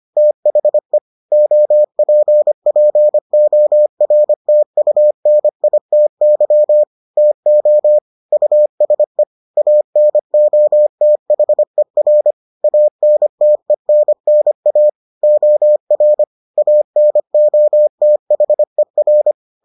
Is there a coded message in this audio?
Yes: Morse code from 0.2 to 19.3 s, heard as a keyed tone, about 25 wpm, 600 Hz. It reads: THE OPPORTUNITY TO USE ANOTHER ANTENNA OR ANOTHER